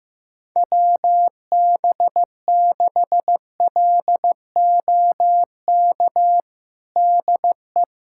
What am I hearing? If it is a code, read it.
Morse code, heard as a keyed tone, about 16 wpm, 700 Hz. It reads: WB6LOK DE
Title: USA Callsign DE Callsign - 6 Characters - 15wpm